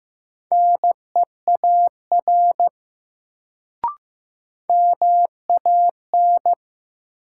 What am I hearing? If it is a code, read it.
Morse code, heard as a keyed tone, about 16 wpm, 700 Hz. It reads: NEAR MAN
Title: Sentences from Top 200 Words - New Word Drill - 15wpm